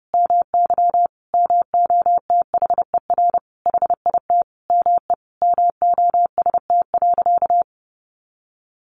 Morse code, heard as a keyed tone, about 30 wpm, 700 Hz. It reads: MY MOTHER HIT ME MOST.
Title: Sentences from Top 1400 Words 30wpm